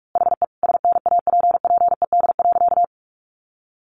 Morse code, heard as a keyed tone, about 44 wpm, 700 Hz. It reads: HE SNAPPED.
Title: A Sound Of Thunder 45wpm